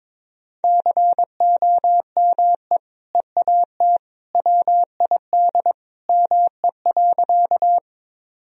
Morse code, heard as a keyed tone, about 21 wpm, 700 Hz. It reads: COME EAT WID ME.
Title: Sentences from Top 300 Words and Abbreviations 22wpm